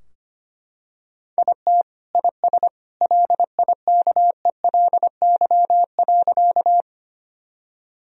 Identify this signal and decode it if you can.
Morse code, heard as a keyed tone, about 25 wpm, 700 Hz. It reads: IT IS LIKELY.
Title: Sentences from Top 1000 Words - Review Part 4 - 25wpm